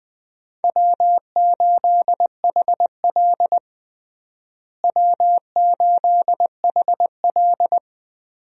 Morse code, heard as a keyed tone, about 20 wpm, 700 Hz. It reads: W8HL W8HL
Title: USA Call Signs - 4 Characters - Repeated 2x - 20wpm